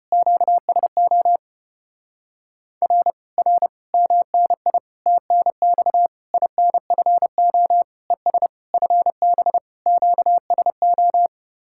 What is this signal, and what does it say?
Morse code, heard as a keyed tone, about 33 wpm, 700 Hz. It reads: QSO R R MNI TNX INFO ES FB QSO